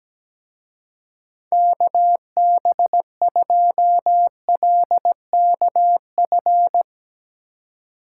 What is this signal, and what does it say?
Morse code, heard as a keyed tone, about 19 wpm, 700 Hz. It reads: KB2LKF
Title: Speed Racing - USA 2x3 Call Signs 17wpm